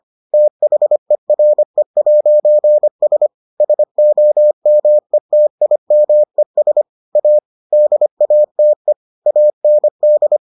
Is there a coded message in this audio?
Yes: Morse code from 0.3 to 10.4 s, heard as a keyed tone, about 25 wpm, 600 Hz. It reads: THERE'S SOMETIMES A DATE AND